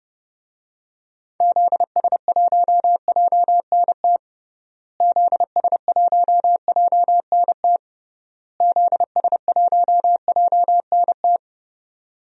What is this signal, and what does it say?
Morse code, heard as a keyed tone, about 30 wpm, 700 Hz. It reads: ZS1JNT ZS1JNT ZS1JNT
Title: DXCC-100 Call Signs - 6 Characters - Repeated 3x - 30wpm